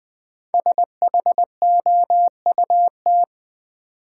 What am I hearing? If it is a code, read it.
Morse code, heard as a keyed tone, about 20 wpm, 700 Hz. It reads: SHOUT
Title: 5 Letter Words - First 3 Letters Given - 20wpm